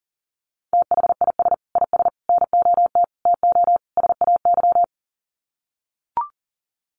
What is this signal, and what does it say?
Morse code, heard as a keyed tone, about 40 wpm, 700 Hz. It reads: THIS IS NOT TO SAY
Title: Sets of 5 Words Encore 40wpm